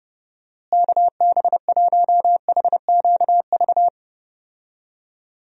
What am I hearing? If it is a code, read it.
Morse code, heard as a keyed tone, about 30 wpm, 700 Hz. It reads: KB1HQV
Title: USA Call Signs - 6 Characters - Repeated 2x - 30wpm